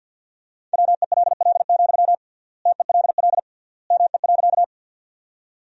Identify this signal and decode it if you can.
Morse code, heard as a keyed tone, about 51 wpm, 700 Hz. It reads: WELL, TELL ME.
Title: 3x Word-Spacing - Sentences from Top 200 Words - 50wpm